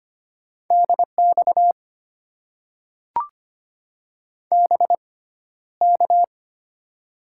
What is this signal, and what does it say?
Morse code, heard as a keyed tone, about 25 wpm, spaced as for 13 wpm, 700 Hz. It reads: DX BK
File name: Top 90 QSO Elements - 6x Character Spacing - 25wpm